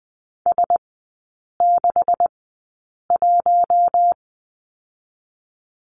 Morse code, heard as a keyed tone, about 20 wpm, spaced as for 8 wpm, 700 Hz. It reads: S61
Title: 127 Lesson 21e - Number 6 - Three Characters 20wpm